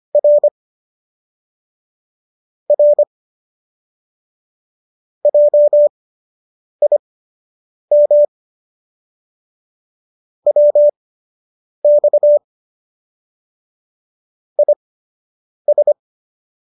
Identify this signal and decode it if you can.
Morse code, heard as a keyed tone, about 25 wpm, spaced as for 8 wpm, 590 Hz. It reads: R R JIM WX IS